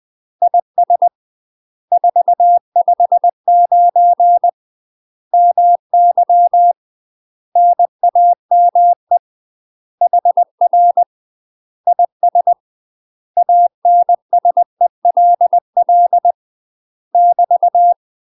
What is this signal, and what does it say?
Morse code, heard as a keyed tone, about 20 wpm, 700 Hz. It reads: IS 459 MY NAME HR IS ANSELL <BT>
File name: UK MASTER QSOs TRAINING / 20WPM / UK MASTER QSOs 0020